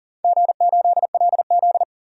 Morse code, heard as a keyed tone, about 40 wpm, 700 Hz. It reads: G8LZ
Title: DXCC-100 Call Signs - 4 Characters - 40wpm